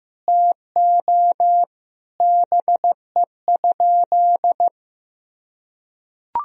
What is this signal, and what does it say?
Morse code, heard as a keyed tone, about 16 wpm, 700 Hz. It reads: TO BE?
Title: Sentences from Top 1000 Words - Review Part 2 - 15wpm